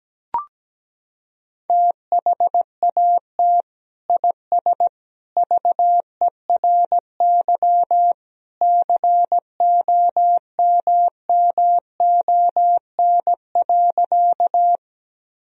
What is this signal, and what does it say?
Morse code, heard as a keyed tone, about 17 wpm, 700 Hz. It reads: THAT IS VERY COMMON.
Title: Sentences from Top 1000 Words - Review Part 3 - 17wpm